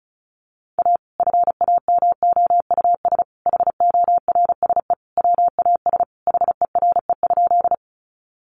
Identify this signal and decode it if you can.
Morse code, heard as a keyed tone, about 34 wpm, 700 Hz. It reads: A FAMOUS HORSE WAS HERE?
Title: Sentences from Top 1500 Words 35wpm